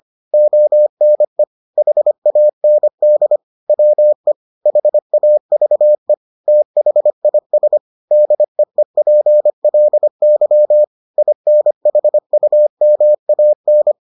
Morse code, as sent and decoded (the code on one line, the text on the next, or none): --- -. / .... .- -. -.. / .-- . / .... .- ...- . / - .... .. ... / -.. . .--. .-.. -.-- / .. -. .... ..- -- .- -.
ONE HAND WE HAVE THIS DEEPLY INHUMAN